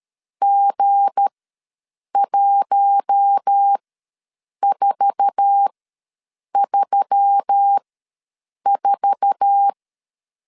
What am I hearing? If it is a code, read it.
Morse code, heard as a keyed tone, about 13 wpm, spaced as for 7 wpm, 800 Hz. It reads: G1434
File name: LEZIONI / 8 YJ45 / 8 YJ4 ws 12wpm 3249